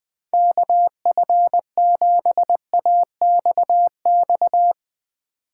Morse code, heard as a keyed tone, about 21 wpm, 700 Hz. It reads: KF7AXX
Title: USA Call Signs - 6 Characters - 2x Character Spacing - 20wpm